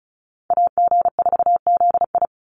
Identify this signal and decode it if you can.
Morse code, heard as a keyed tone, about 34 wpm, 700 Hz. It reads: AG4ZI
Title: USA Call Signs 35wpm